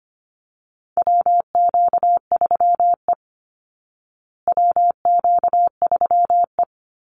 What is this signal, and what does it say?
Morse code, heard as a keyed tone, about 25 wpm, 700 Hz. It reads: WQ3E WQ3E
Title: Call Signs Repeated 3x - 25wpm